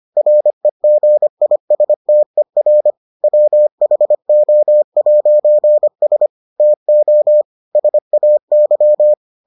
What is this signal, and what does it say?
Morse code, heard as a keyed tone, about 25 wpm, 600 Hz. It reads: REGISTER WHO'S TO SAY